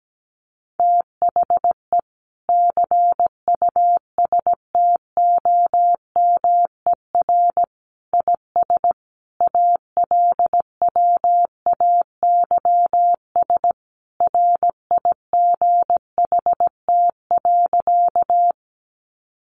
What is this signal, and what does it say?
Morse code, heard as a keyed tone, about 17 wpm, 700 Hz. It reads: THE CUSTOMER IS ALWAYS RIGHT.
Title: Sentences from Top 1100 Words 17wpm